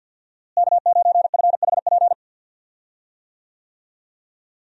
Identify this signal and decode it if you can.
Morse code, heard as a keyed tone, about 52 wpm, 700 Hz. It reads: K9FHP
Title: USA Call Signs - 5 Characters - Repeated 3x - 50wpm